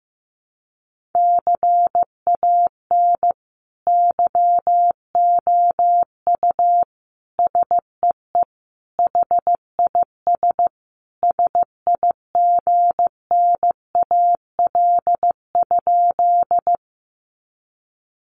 Morse code, heard as a keyed tone, about 15 wpm, 700 Hz. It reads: CAN YOU SEE HIS SIGNAL?
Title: Sentences from Top 1400 Words 15wpm